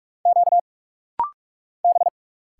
Morse code, heard as a keyed tone, about 46 wpm, 700 Hz. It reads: Q B
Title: Single Letter-Number Alt - Rapid-Fire 45wpm